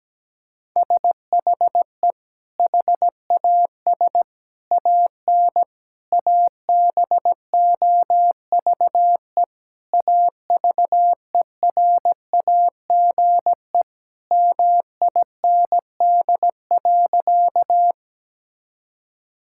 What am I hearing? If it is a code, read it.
Morse code, heard as a keyed tone, about 17 wpm, 700 Hz. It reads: SHE HAS AN ABOVE AVERAGE MIND.
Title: Sentences from Top 1000 Words - Review Part 6 - 17wpm